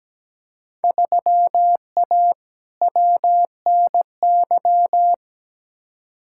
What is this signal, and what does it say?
Morse code, heard as a keyed tone, about 17 wpm, 700 Hz. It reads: 3A WNY
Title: Field Day Exchanges - 17wpm